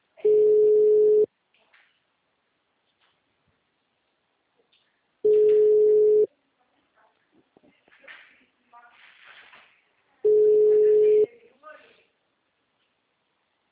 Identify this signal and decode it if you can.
Ringback tone (Europe, 425 Hz), with speech.
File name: out- -2006- - - .384